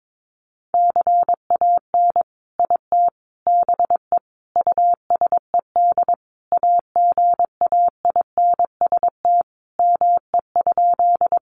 Morse code, heard as a keyed tone, about 22 wpm, 700 Hz. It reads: CAN IT BE USED AGAINST ME?
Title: Sentences from Top 600 Words 22wpm